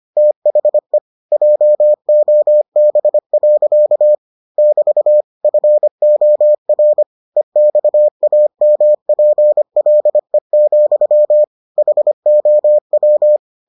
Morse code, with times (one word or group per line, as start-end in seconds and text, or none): 0.2-1.0 THE
1.3-4.2 JOB.
4.6-5.2 <BT>
5.4-7.0 FOR
7.4-11.4 EXAMPLE,
11.8-13.4 HOW